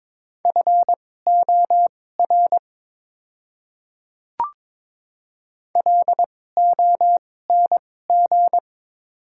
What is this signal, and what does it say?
Morse code, heard as a keyed tone, about 22 wpm, spaced as for 16 wpm, 700 Hz. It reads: FOR LONG
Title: 2x Character Spacing - Top 300 Words - 22wpm